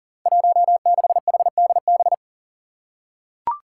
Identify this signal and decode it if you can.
Morse code, heard as a keyed tone, about 39 wpm, 700 Hz. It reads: 16HDB
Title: Five Challenging Letter-Number Combinations 40wpm